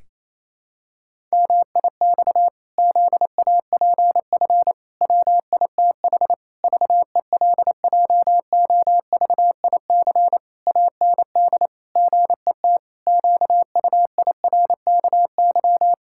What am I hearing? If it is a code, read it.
Morse code, heard as a keyed tone, about 28 wpm, 700 Hz. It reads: MIX ZAPF WITH VELJOVIC AND GET QUIRKY
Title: Pangram Challenge v2 - 28wpm